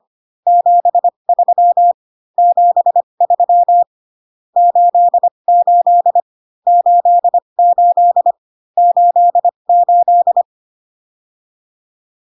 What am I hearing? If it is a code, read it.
Morse code, heard as a keyed tone, about 25 wpm, spaced as for 22 wpm, 700 Hz. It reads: 73 73 88 88 88
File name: CW common abbreviations alphabetical repeated 3 times 25 18wpm